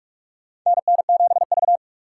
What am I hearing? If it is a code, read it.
Morse code, heard as a keyed tone, about 42 wpm, 690 Hz. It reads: NN7V
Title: USA Call Signs - 4 Characters - 8x Character Spacing - 45wpm